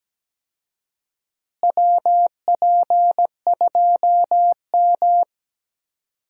Morse code, heard as a keyed tone, about 17 wpm, 700 Hz. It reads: WP2M
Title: USA Call Signs - 4 Characters - Repeated 3x - 17wpm